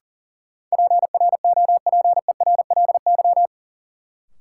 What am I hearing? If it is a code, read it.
Morse code, heard as a keyed tone, about 40 wpm, 700 Hz. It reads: PROPERLY
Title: Sentences from Top 1100 Words - New Word Drill - 40wpm